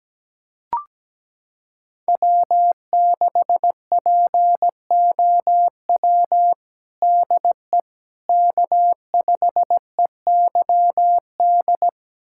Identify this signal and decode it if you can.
Morse code, heard as a keyed tone, about 17 wpm, 700 Hz. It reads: W6POW DE K5EYD
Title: USA Callsign DE Callsign 17wpm